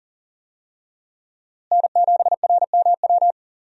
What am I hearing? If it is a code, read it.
Morse code, heard as a keyed tone, about 40 wpm, 700 Hz. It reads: N7RMW
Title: USA Call Signs - 5 Characters - Repeated 3x - 40wpm